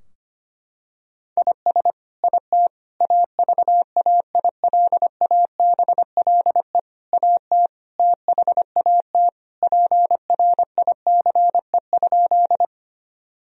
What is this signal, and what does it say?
Morse code, heard as a keyed tone, about 25 wpm, 700 Hz. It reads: IS IT AVAILABLE AT THAT PRICE?
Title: Sentences from Top 1000 Words - Review Part 3 - 25wpm